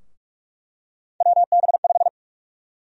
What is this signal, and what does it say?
Morse code, heard as a keyed tone, about 46 wpm, 700 Hz. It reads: WB5
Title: USA Call Sign Prefixes - Repeated 2x - 45wpm